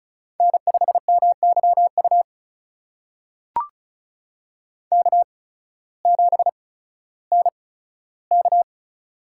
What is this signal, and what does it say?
Morse code, heard as a keyed tone, about 34 wpm, spaced as for 16 wpm, 700 Hz. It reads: N5MYU K7NK
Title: USA Call Signs - 8x Character Spacing - 35wpm